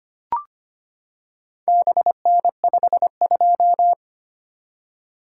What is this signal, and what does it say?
Morse code, heard as a keyed tone, about 25 wpm, 700 Hz. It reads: BN52
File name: practice / 25wpm / Four Letter-Number Combinations - 25wpm